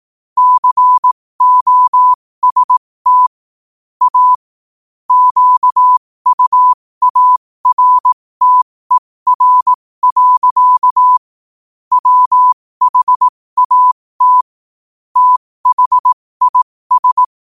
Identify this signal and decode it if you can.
Morse code, heard as a keyed tone, about 18 wpm, spaced as for 15 wpm, 1000 Hz. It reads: COST A QUARTER. WHAT THIS
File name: qotdfiles / 15wpm